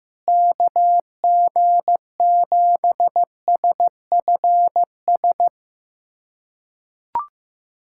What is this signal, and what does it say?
Morse code, heard as a keyed tone, about 15 wpm, 700 Hz. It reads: KG7SFS
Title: USA Callsign DE Callsign 15wpm